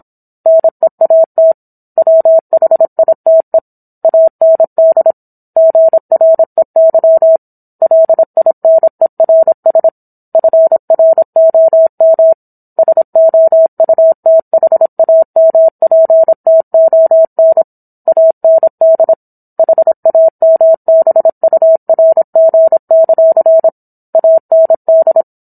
Morse code, heard as a keyed tone, about 27 wpm, 650 Hz. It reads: NEAT WHITE AND GREY LINERS FROM SOUTHAMPTON AND HAMBURG; AND
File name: Book One / 17-Chapter 17 - 26 WPM